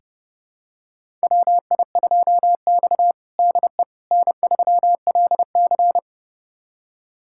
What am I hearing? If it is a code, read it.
Morse code, heard as a keyed tone, about 32 wpm, 700 Hz. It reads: WI2X DE N3LC
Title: USA Callsign DE Callsign - 4 Characters - 30wpm